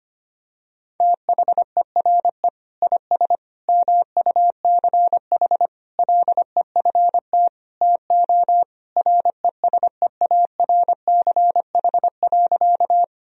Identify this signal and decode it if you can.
Morse code, heard as a keyed tone, about 25 wpm, 700 Hz. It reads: THERE IS MUCH LEFT TO RESEARCH.
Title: Sentences from Top 1000 Words - Review Part 4 - 25wpm